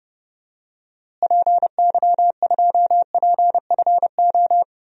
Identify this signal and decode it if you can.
Morse code, heard as a keyed tone, about 30 wpm, 700 Hz. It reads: PY2PFO
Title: DXCC-100 Call Signs - 6 Characters - Repeated 2x - 30wpm